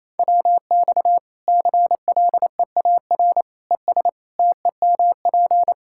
Morse code, heard as a keyed tone, about 28 wpm, 700 Hz. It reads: WX CLEAR ES TEMP
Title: Practice QSO - Exchange 2 - Info - 28wpm